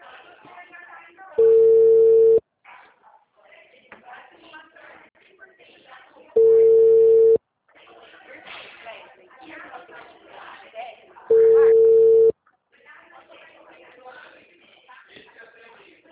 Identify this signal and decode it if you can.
Ringback tone (China, 450 Hz), with speech.